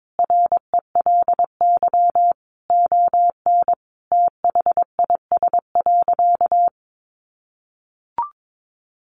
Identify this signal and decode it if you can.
Morse code, heard as a keyed tone, about 22 wpm, 700 Hz. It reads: RELY ON THIS.
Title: Sentences from Top 1200 Words 22wpm